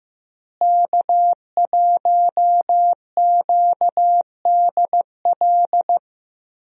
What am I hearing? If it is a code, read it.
Morse code, heard as a keyed tone, about 15 wpm, 700 Hz. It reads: K1QDL
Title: USA Call Signs - 5 Characters - Repeated 3x - 15wpm